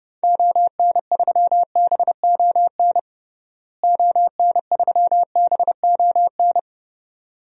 Morse code, heard as a keyed tone, about 30 wpm, 700 Hz. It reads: ON3BON ON3BON
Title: DXCC-100 Call Signs - 6 Characters - Repeated 2x - 30wpm